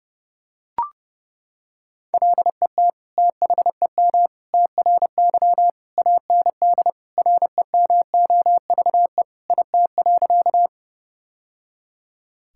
Morse code, heard as a keyed tone, about 32 wpm, 700 Hz. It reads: LET THEM TRY AND REMOVE IT.
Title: Sentences from Top 1000 Words - Review Part 5 - 30wpm